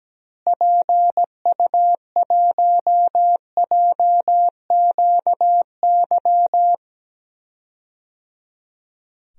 Morse code, heard as a keyed tone, about 17 wpm, 700 Hz. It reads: PU1JQY
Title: DXCC-100 Call Signs - 6 Characters - Repeated 3x - 17wpm